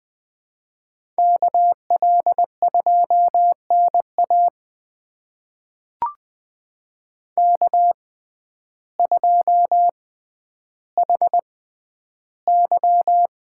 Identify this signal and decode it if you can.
Morse code, heard as a keyed tone, about 20 wpm, spaced as for 11 wpm, 700 Hz. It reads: KL2NA K2HY
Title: USA Call Signs - 5 Characters - 6x Character Spacing - 20wpm